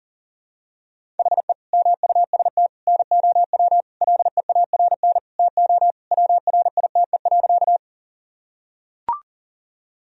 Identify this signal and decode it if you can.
Morse code, heard as a keyed tone, about 40 wpm, 700 Hz. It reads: HE MUST NOW LEARN TO WRITE.
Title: Sentences from Top 1000 Words - Review Part 1 - 40wpm